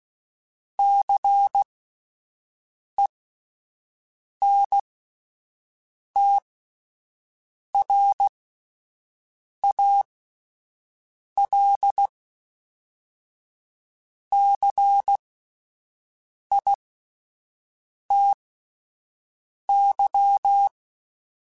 Morse code, heard as a keyed tone, about 16 wpm, spaced as for 5 wpm, 800 Hz. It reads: CENTRALCITY